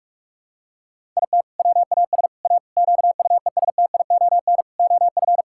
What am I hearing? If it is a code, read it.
Morse code, heard as a keyed tone, about 41 wpm, 700 Hz. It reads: IT WAS A QUESTION OF
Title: Sentences from Top 1000 Words - Review Part 6 - 45wpm